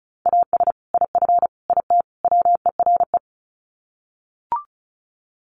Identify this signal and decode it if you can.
Morse code, heard as a keyed tone, about 34 wpm, 700 Hz. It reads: AS IF IT WERE
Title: Sets of 4 Words 35wpm